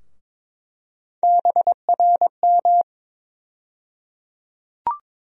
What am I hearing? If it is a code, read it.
Morse code, heard as a keyed tone, about 22 wpm, 700 Hz. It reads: BRM
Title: USA Call Sign Postfixes - 3 Letters 22wpm